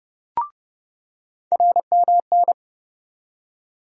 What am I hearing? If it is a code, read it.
Morse code, heard as a keyed tone, about 31 wpm, 700 Hz. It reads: RMN
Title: USA Call Sign Postfixes - 3 Letters 30wpm